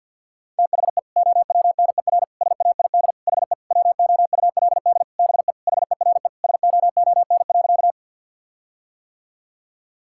Morse code, heard as a keyed tone, about 50 wpm, 700 Hz. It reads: THE OWNER SAID HE WOULD BE HERE SOON.
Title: Sentences from Top 1600 Words 50wpm